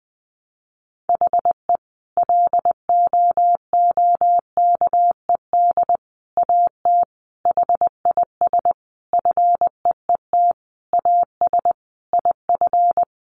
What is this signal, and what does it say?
Morse code, heard as a keyed tone, about 20 wpm, 700 Hz. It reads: HE LOOKED AT HIS FEET AS IF
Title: A Sound Of Thunder 20wpm